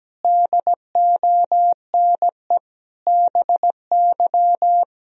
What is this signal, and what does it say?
Morse code, heard as a keyed tone, about 17 wpm, 700 Hz. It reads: DONE BY
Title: Sentences from Top 1000 Words - Review Part 3 - 17wpm